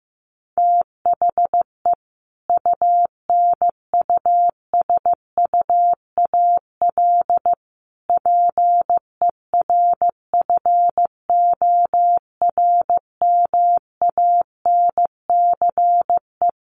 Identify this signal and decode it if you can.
Morse code, heard as a keyed tone, about 15 wpm, 700 Hz. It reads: THE UNUSUAL PERFORMANCE
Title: Sentences from Top 1400 Words 15wpm